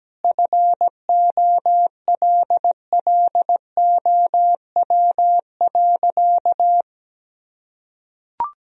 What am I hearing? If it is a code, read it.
Morse code, heard as a keyed tone, about 17 wpm, 700 Hz. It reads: FOLLOW.
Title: Sentences from Top 1000 Words - Review Part 1 - 17wpm